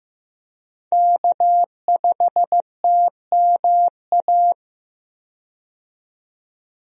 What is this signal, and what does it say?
Morse code, heard as a keyed tone, about 15 wpm, 700 Hz. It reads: K5TMA
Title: USA Call Signs - 5 Characters - 15wpm